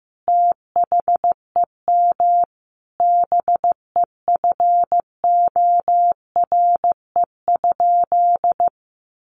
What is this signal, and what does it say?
Morse code, heard as a keyed tone, about 15 wpm, 700 Hz. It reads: THEM BEFORE?
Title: Sentences from Top 900 Words 15wpm